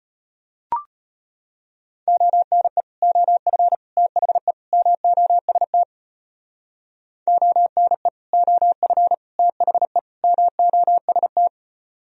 Morse code, heard as a keyed tone, about 31 wpm, 700 Hz. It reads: ONE OF THE MOST ONE OF THE MOST